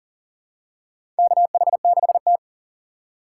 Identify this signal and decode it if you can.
Morse code, heard as a keyed tone, about 40 wpm, 700 Hz. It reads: KH6T
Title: USA Call Signs - 4 Characters - 3x Character Spacing - 40wpm